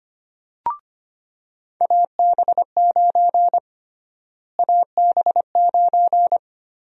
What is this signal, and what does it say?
Morse code, heard as a keyed tone, about 25 wpm, 700 Hz. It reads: AB9 AB9
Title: USA Call Sign Prefixes - Repeated 2x - 25wpm